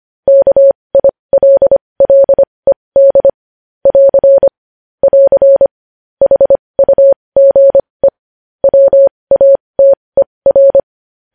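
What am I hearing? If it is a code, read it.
Morse code, heard as a keyed tone, about 25 wpm, spaced as for 20 wpm, 550 Hz. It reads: KILLED <AR> <AR> HUGE WATER
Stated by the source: Morse news headlines sent at 20.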